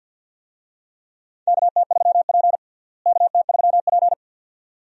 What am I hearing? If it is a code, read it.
Morse code, heard as a keyed tone, about 49 wpm, 700 Hz. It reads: KT3P KT3P
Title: USA Call Signs - 4 Characters - Repeated 2x - 50wpm